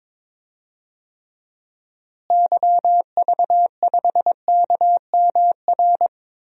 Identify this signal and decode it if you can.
Morse code, heard as a keyed tone, about 22 wpm, 700 Hz. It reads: YV5KMR
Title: DXCC-100 Call Signs - 6 Characters - Repeated 3x - 22wpm